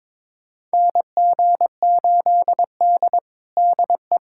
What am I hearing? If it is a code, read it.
Morse code, heard as a keyed tone, about 22 wpm, 700 Hz. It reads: NG8D DE
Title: USA Callsign DE Callsign - 4 Characters - 22wpm